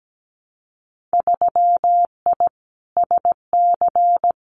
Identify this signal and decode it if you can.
Morse code, heard as a keyed tone, about 17 wpm, 700 Hz. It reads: 3I SC
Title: Winter Field Day Exchanges Encore 17wpm